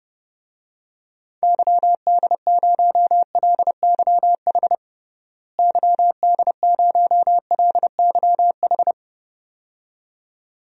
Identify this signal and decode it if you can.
Morse code, heard as a keyed tone, about 30 wpm, 700 Hz. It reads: YD0LYH YD0LYH
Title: DXCC-100 Call Signs - 6 Characters - Repeated 2x - 30wpm